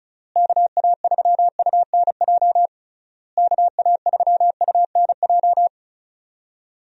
Morse code, heard as a keyed tone, about 36 wpm, 700 Hz. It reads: KA3UNJ KA3UNJ
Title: USA Call Signs - 6 Characters - Repeated 3x - 35wpm